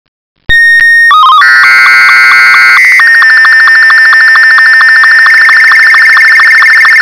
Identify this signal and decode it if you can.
SSTV (slow-scan television), Martin 2.
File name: shortwave audio / SSTV Marti 2